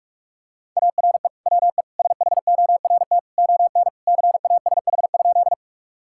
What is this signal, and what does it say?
Morse code, heard as a keyed tone, about 45 wpm, 700 Hz. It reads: ARE WE SHORT ON CASH?
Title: Sentences from Top 1000 Words - Review Part 5 - 45wpm